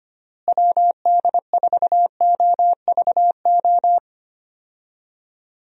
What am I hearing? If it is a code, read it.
Morse code, heard as a keyed tone, about 25 wpm, 700 Hz. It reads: WD4OVO
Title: USA Call Signs - 6 Characters - Repeated 2x - 25wpm